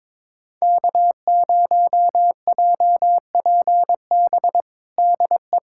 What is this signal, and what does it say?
Morse code, heard as a keyed tone, about 22 wpm, 700 Hz. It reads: K0JPB DE
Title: USA Callsign DE Callsign - 5 Characters - 22wpm